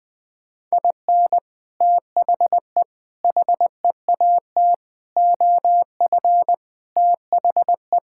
Morse code, heard as a keyed tone, about 20 wpm, 700 Hz. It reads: IN THE HEAT OF THE
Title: Sets of 6 Words - 20wpm